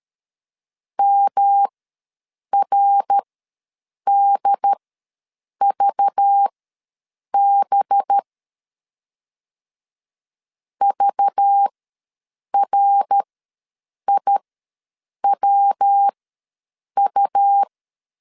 Morse code, heard as a keyed tone, about 13 wpm, spaced as for 7 wpm, 800 Hz. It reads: MRDVB VRIWU
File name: LEZIONI / 5 LBVC / 5 LBV ws 12wpm 3015